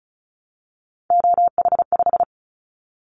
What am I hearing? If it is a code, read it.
Morse code, heard as a keyed tone, about 34 wpm, 700 Hz. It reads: OH5
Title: Callsign Prefixes for Top 50 DXCC - 35wpm